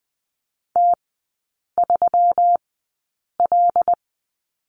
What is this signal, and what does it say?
Morse code, heard as a keyed tone, about 20 wpm, spaced as for 8 wpm, 700 Hz. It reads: T3L
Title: 121 Lesson 20e - Number 3 - Three Characters 20wpm